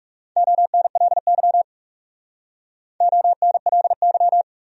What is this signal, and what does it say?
Morse code, heard as a keyed tone, about 35 wpm, 700 Hz. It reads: ONLY ONLY
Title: Speed-Racing Top 300 Words 30wpm